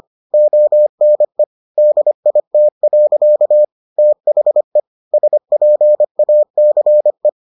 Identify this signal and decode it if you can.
Morse code, heard as a keyed tone, about 25 wpm, 600 Hz. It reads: ONE DIT. THE SPACE